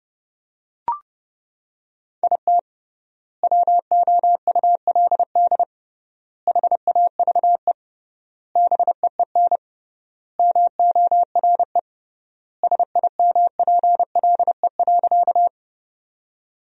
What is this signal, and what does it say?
Morse code, heard as a keyed tone, about 30 wpm, 700 Hz. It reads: IT WOULD HAVE BEEN MORE SIMPLE.
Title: 3x Word-Spacing - Sentences from Top 400 Words - 30wpm